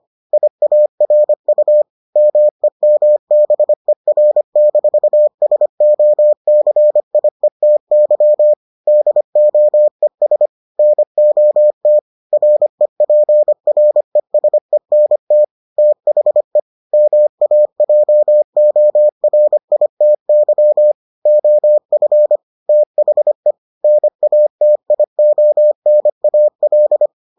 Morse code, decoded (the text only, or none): IARU MEMBER-SOCIETY DOES NOT REPRESENT THE MAJORITY OF THE NATIONAL